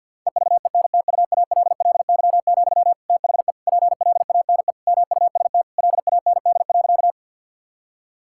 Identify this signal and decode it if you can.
Morse code, heard as a keyed tone, about 50 wpm, 700 Hz. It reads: EVENTUALLY, THE PLANE MUST LAND.
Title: Sentences from Top 1000 Words - Review Part 6 - 50wpm